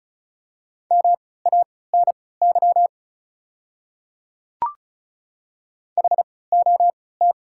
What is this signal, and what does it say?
Morse code, heard as a keyed tone, about 36 wpm, spaced as for 20 wpm, 690 Hz. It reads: MANY HOT
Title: 3x Character Spacing - Top 100 Words - 35wpm